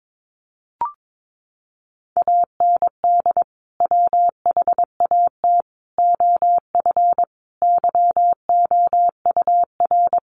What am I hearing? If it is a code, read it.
Morse code, heard as a keyed tone, about 22 wpm, 700 Hz. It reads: AND WHAT OF YOUR
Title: Sentences from Top 900 Words 22wpm